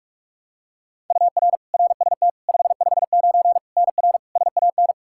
Morse code, heard as a keyed tone, about 44 wpm, 700 Hz. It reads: UR RST 559 NR SAN
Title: K1USN SST QSO Practice 45wpm